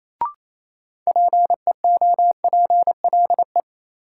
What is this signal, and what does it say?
Morse code, heard as a keyed tone, about 28 wpm, 700 Hz. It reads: PEOPLE
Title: Sentences from Top 200 Words - New Word Drill - 28wpm